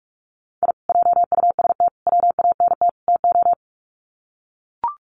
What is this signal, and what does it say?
Morse code, heard as a keyed tone, about 44 wpm, 700 Hz. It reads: I JUST WANT TO
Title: Sets of 4 Words 45wpm